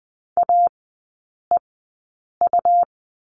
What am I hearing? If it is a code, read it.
Morse code, heard as a keyed tone, about 20 wpm, spaced as for 9 wpm, 690 Hz. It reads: AEU
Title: 145 Lesson 24e - Letter y - Three Characters 20wpm